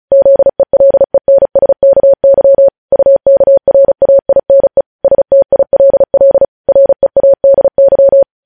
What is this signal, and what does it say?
Morse code, heard as a keyed tone, about 36 wpm, 550 Hz. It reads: ZELENSKY UKRAINE STILL READY
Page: News Headlines in Morse Code at 35 WPM